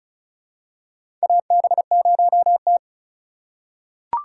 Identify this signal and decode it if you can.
Morse code, heard as a keyed tone, about 36 wpm, 700 Hz. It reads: AB0T
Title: USA Call Signs - 4 Characters - 3x Character Spacing - 35wpm